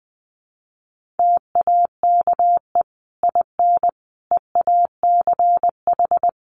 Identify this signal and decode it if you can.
Morse code, heard as a keyed tone, about 20 wpm, 700 Hz. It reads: TAKE IN EACH
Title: Sentences from Top 1400 Words 20wpm